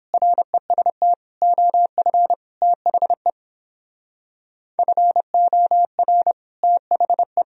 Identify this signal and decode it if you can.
Morse code, heard as a keyed tone, about 29 wpm, 700 Hz. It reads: REST OF THE FOR THE